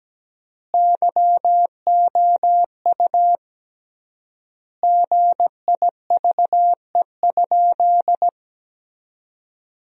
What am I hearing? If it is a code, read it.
Morse code, heard as a keyed tone, about 17 wpm, 700 Hz. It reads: YOU GIVE?
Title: 3x Word-Spacing - Sentences from Top 200 Words - 17wpm